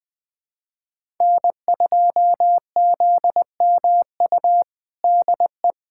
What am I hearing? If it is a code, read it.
Morse code, heard as a keyed tone, about 20 wpm, 700 Hz. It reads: N2ZMU DE
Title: USA Callsign DE Callsign 20wpm